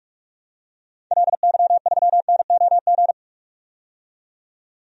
Morse code, heard as a keyed tone, about 46 wpm, 700 Hz. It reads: LY3NOG
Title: DXCC-100 Call Signs - 6 Characters - 45wpm